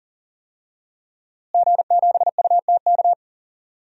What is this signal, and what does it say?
Morse code, heard as a keyed tone, about 40 wpm, 700 Hz. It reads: G7UTK
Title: DXCC-100 Call Signs - 5 Characters - Repeated 2x - 40wpm